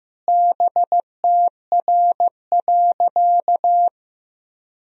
Morse code, heard as a keyed tone, about 15 wpm, 700 Hz. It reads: BTR.
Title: Sentences from Top 400 Words and Abbreviations 15wpm